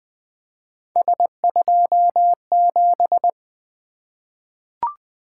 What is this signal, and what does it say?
Morse code, heard as a keyed tone, about 20 wpm, 700 Hz. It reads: S27
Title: Three Letter-Number Combinations - 20wpm